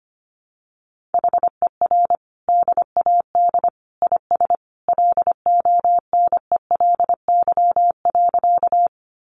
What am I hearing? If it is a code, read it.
Morse code, heard as a keyed tone, about 25 wpm, 700 Hz. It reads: HER DAD IS LONELY.
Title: Additional Easy Phrases 25wpm